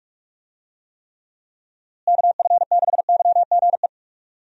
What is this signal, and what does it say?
Morse code, heard as a keyed tone, about 46 wpm, 700 Hz. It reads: KF6YGE
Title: USA Call Signs - 6 Characters - Repeated 2x - 45wpm